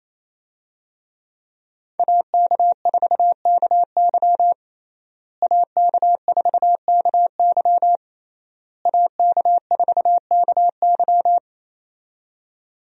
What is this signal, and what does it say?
Morse code, heard as a keyed tone, about 28 wpm, 700 Hz. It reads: AK4KY AK4KY AK4KY
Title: USA Call Signs - 5 Characters - Repeated 3x - 28wpm